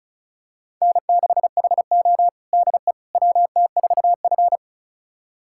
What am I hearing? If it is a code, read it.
Morse code, heard as a keyed tone, about 36 wpm, 700 Hz. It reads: N6HO DE WT4F
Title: USA Callsign DE Callsign - 4 Characters - 35wpm